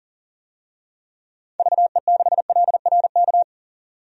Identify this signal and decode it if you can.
Morse code, heard as a keyed tone, about 40 wpm, 700 Hz. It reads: VE6LRK
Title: DXCC-100 Call Signs - 6 Characters - 40wpm